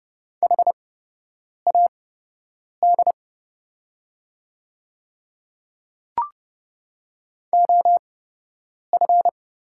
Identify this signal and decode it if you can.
Morse code, heard as a keyed tone, about 30 wpm, spaced as for 8 wpm, 700 Hz. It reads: HAD OF
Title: 8x Character Spacing - Top 100 Words - 30wpm